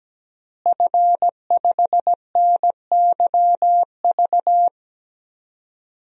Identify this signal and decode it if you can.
Morse code, heard as a keyed tone, about 17 wpm, 700 Hz. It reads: F5NYV
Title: DXCC-100 Call Signs - 5 Characters - Repeated 3x - 17wpm